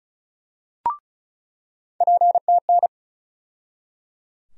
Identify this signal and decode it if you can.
Morse code, heard as a keyed tone, about 34 wpm, 700 Hz. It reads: PTN
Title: USA Call Sign Postfixes - 3 Letters 35wpm